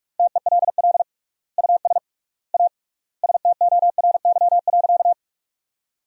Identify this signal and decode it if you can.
Morse code, heard as a keyed tone, about 43 wpm, 700 Hz. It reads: TELL US A STORY.
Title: 3x Word-Spacing - Sentences from Top 300 Words - 45wpm